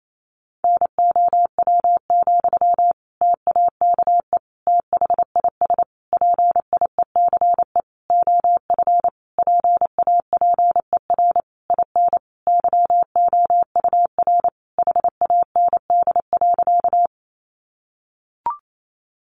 Morse code, heard as a keyed tone, about 28 wpm, 700 Hz. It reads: NOW, TAKE THIS PIECE OF PAPER IN YOUR HAND.
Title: Sentences from Top 400 Words 28wpm